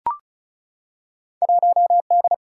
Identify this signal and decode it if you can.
Morse code, heard as a keyed tone, about 36 wpm, 700 Hz. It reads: 1D
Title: Field Day Exchanges - 35wpm